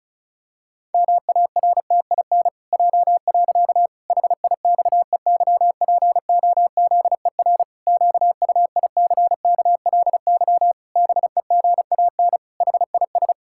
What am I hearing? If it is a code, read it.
Morse code, heard as a keyed tone, about 36 wpm, 700 Hz. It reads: MARTIN J. HIXEYPOZER QUICKLY BEGAN HIS
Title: Pangram Challenge v2 - 35wpm